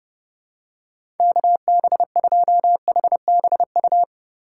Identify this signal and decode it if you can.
Morse code, heard as a keyed tone, about 30 wpm, 700 Hz. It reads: KB2HBU
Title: USA Call Signs - 6 Characters - 4x Character Spacing - 30wpm